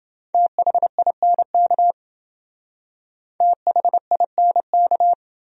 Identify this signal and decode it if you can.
Morse code, heard as a keyed tone, about 28 wpm, 700 Hz. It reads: THINK THINK